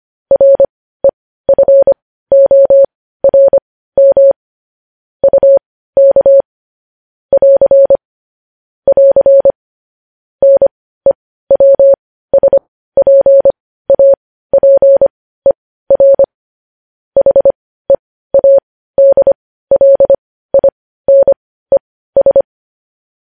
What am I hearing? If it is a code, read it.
Morse code, heard as a keyed tone, about 25 wpm, spaced as for 15 wpm, 550 Hz. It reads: REFORM UK <AR> <AR> NEWSPAPER HEADLINES
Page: News Headlines in Morse Code at 15 WPM